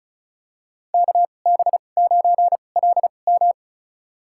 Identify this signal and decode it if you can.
Morse code, heard as a keyed tone, about 36 wpm, spaced as for 25 wpm, 700 Hz. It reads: KB9LM
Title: USA Call Signs - 2x Character Spacing - 35wpm